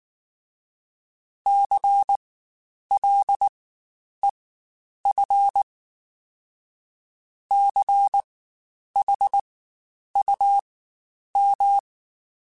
Morse code, heard as a keyed tone, about 19 wpm, spaced as for 9 wpm, 800 Hz. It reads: CLEF CHUM
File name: Spaced Alphas / Spaced Alphas mp3s / Cs sp @18